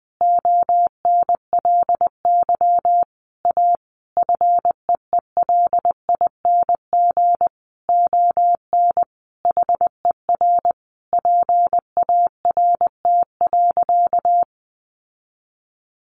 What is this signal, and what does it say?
Morse code, heard as a keyed tone, about 20 wpm, 700 Hz. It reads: ONLY A FEELING ON HER PART.
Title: Sentences from Top 1200 Words 20wpm